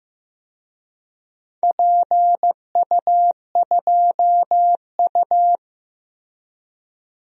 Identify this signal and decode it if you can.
Morse code, heard as a keyed tone, about 15 wpm, 700 Hz. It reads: PU2U
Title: DXCC-100 Call Signs - 4 Characters - Repeated 3x - 15wpm